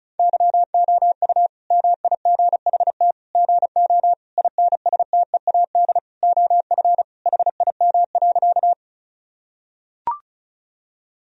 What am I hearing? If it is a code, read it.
Morse code, heard as a keyed tone, about 36 wpm, 700 Hz. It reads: YOU MIGHT GO INSTEAD OF HIM.
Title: Sentences from Top 1000 Words - Review Part 4 - 35wpm